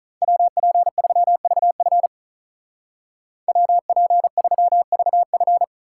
Morse code, heard as a keyed tone, about 43 wpm, 700 Hz. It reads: WP3VF WP3VF
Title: Speed Racing - USA 2x2 Call Signs 35wpm